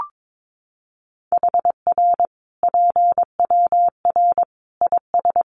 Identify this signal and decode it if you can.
Morse code, heard as a keyed tone, about 22 wpm, 700 Hz. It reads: HR PWR IS
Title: Common QSO Phrases 22wpm